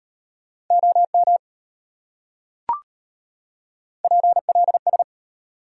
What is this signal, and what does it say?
Morse code, heard as a keyed tone, about 33 wpm, 700 Hz. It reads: OM PLS